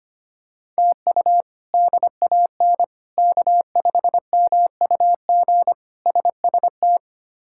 Morse code, heard as a keyed tone, about 25 wpm, 700 Hz. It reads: TU DAN K5MUG SST
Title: K1USN SST QSO Practice - 25wpm